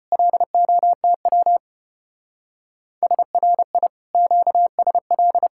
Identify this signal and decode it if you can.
Morse code, heard as a keyed tone, about 30 wpm, 700 Hz. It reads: LOTW SRI QSL
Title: Speed-Racing Common QSO Phrases - 25wpm